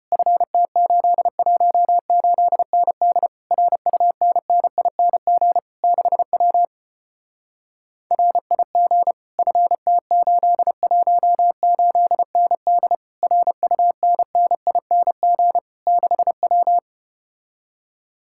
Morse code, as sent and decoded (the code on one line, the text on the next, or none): ..-. - ---.. .---- ---.. -. -.. / .-. ..- -. -. .. -. --. / -.... .-- / .-. .. --. / ..-. - ---.. .---- ---.. -. -.. / .-. ..- -. -. .. -. --. / -.... .--
FT818ND RUNNING 6W RIG FT818ND RUNNING 6W